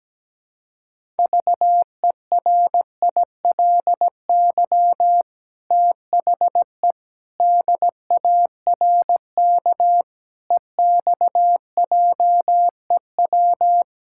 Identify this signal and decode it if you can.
Morse code, heard as a keyed tone, about 17 wpm, 700 Hz. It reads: VERILY THE DARK EXJEW
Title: Pangram Challenge v2 - 17wpm